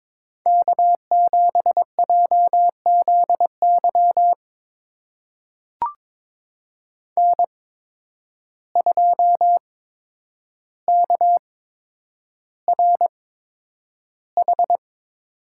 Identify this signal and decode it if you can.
Morse code, heard as a keyed tone, about 22 wpm, spaced as for 9 wpm, 700 Hz. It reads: K7JZY N2KRH
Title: USA Call Signs - 5 Characters - 8x Character Spacing - 22wpm